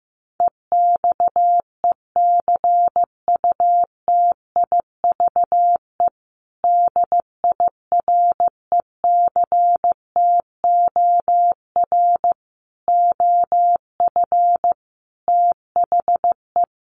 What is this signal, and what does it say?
Morse code, heard as a keyed tone, about 15 wpm, 700 Hz. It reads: EXECUTIVE DIRECTOR OF THE
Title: Sets of 5 Words Encore2 15wpm